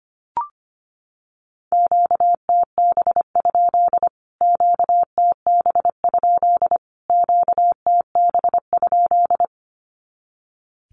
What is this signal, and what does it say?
Morse code, heard as a keyed tone, about 25 wpm, 700 Hz. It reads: QTB? QTB? QTB?
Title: Q-Codes Repeated 3x 25wpm